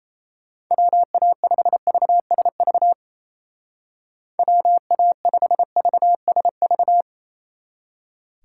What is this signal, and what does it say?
Morse code, heard as a keyed tone, about 36 wpm, 700 Hz. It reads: WA5VSV WA5VSV